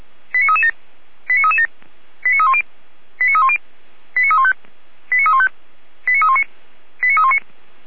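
Five-tone selective call (ZVEI), heard as quick runs of five tones, tones of about 80 ms. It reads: 89337 89337 89310 89310 89315 89315 89319 89319
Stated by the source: ZVEI five-tone selective call received on an SDR.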